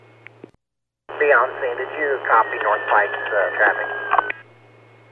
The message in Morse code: none

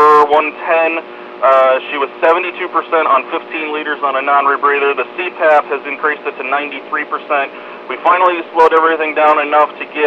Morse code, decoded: none

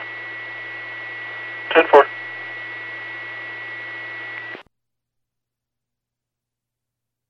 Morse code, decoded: none

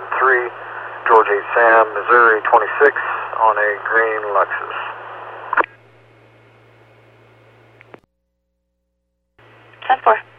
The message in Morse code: none

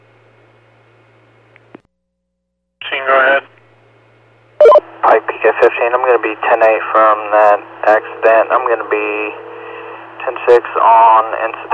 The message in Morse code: none

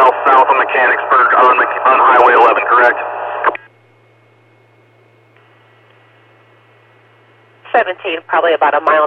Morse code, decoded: none

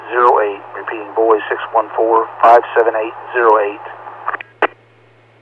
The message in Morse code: none

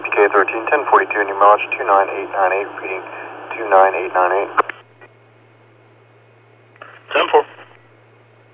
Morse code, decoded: none

